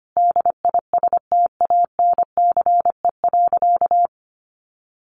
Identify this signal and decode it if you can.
Morse code, heard as a keyed tone, about 25 wpm, 700 Hz. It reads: DISTANCE.
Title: Sentences from Top 1500 Words 25wpm